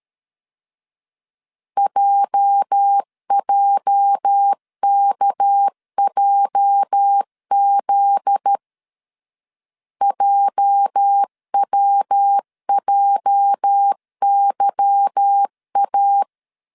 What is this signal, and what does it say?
Morse code, heard as a keyed tone, about 13 wpm, 800 Hz. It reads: JJKJZ JWJYA
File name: LEZIONI / 8 YJ45 / 8 YJ 12wpm 2235